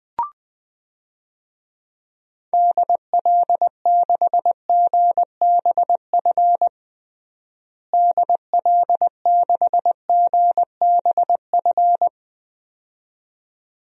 Morse code, heard as a keyed tone, about 20 wpm, 700 Hz. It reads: DL6GBF DL6GBF
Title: DXCC-100 Call Signs - 6 Characters - Repeated 2x - 20wpm